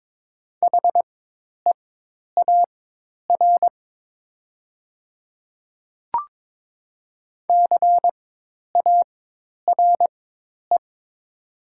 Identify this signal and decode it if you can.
Morse code, heard as a keyed tone, about 21 wpm, spaced as for 10 wpm, 700 Hz. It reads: HEAR CARE
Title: 4x Character Spacing - Top 500 Words - 22wpm